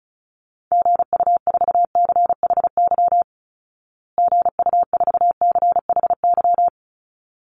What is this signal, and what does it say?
Morse code, heard as a keyed tone, about 34 wpm, 700 Hz. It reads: GU4CHY GU4CHY
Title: Call Signs Repeated 3x - 35wpm